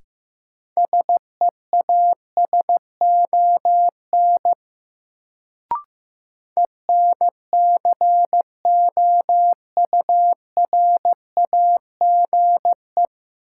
Morse code, heard as a keyed tone, about 15 wpm, 700 Hz. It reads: SEASON ENCOURAGE
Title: Sentences from Top 1400 Words - New Word Drill Alt - 15wpm